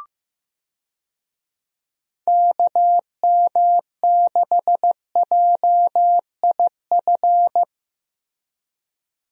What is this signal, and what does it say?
Morse code, heard as a keyed tone, about 15 wpm, 700 Hz. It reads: KM6JIF
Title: USA Call Signs - 6 Characters - Repeated 2x - 15wpm